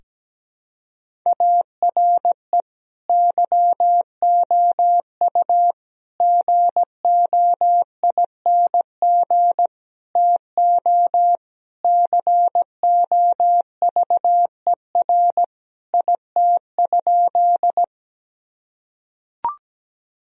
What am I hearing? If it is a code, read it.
Morse code, heard as a keyed tone, about 17 wpm, 700 Hz. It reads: ARE YOU GOING TO COVER IT?
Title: Sentences from Top 1000 Words - Review Part 1 - 17wpm